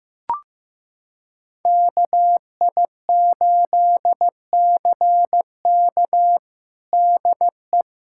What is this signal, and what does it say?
Morse code, heard as a keyed tone, about 15 wpm, 700 Hz. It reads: KI8CK DE
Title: USA Callsign DE Callsign - 5 Characters - 15wpm